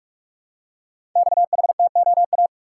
Morse code, heard as a keyed tone, about 44 wpm, 700 Hz. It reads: XHTOA
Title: Code Groups - 5 Letters - 45wpm